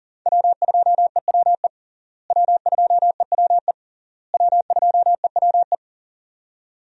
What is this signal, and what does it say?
Morse code, heard as a keyed tone, about 37 wpm, 700 Hz. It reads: W2EWE W2EWE W2EWE
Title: USA Call Signs - 5 Characters - Repeated 3x - 40wpm